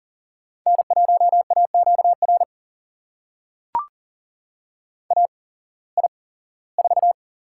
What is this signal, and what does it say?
Morse code, heard as a keyed tone, about 42 wpm, spaced as for 21 wpm, 700 Hz. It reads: N1AQR AI4
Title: USA Call Signs - 8x Character Spacing - 40wpm